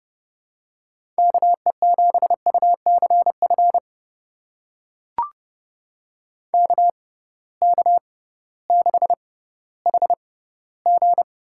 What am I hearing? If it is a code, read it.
Morse code, heard as a keyed tone, about 30 wpm, spaced as for 16 wpm, 700 Hz. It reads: KE7UCF KK6HG
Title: USA Call Signs - 6 Characters - 6x Character Spacing - 30wpm